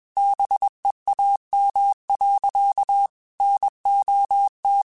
Morse code, heard as a keyed tone, about 21 wpm, 800 Hz. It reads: BEAM. NOT
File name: QSOs / QSOs-mp3s / WD0BGP 1@20